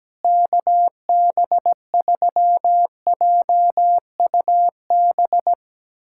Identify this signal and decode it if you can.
Morse code, heard as a keyed tone, about 17 wpm, 700 Hz. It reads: KB3JUB
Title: USA Callsign DE Callsign - 6 Characters - 17wpm